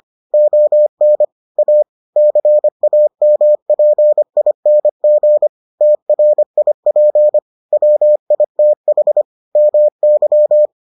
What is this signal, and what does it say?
Morse code, heard as a keyed tone, about 25 wpm, 600 Hz. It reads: ON A CAMPING TRIP WITH MY